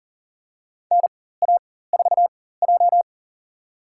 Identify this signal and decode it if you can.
Morse code, heard as a keyed tone, about 42 wpm, spaced as for 19 wpm, 700 Hz. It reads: NA4J
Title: USA Call Signs - 4 Characters - 4x Character Spacing - 40wpm